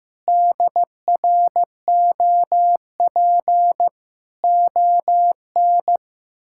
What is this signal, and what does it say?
Morse code, heard as a keyed tone, about 15 wpm, 700 Hz. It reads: DROP ON
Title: Pangram Challenge v2 - 15wpm